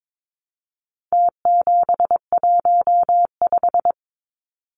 Morse code, heard as a keyed tone, about 22 wpm, 700 Hz. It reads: T715
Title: Numbers - Contest Style - 4-Digit 22wpm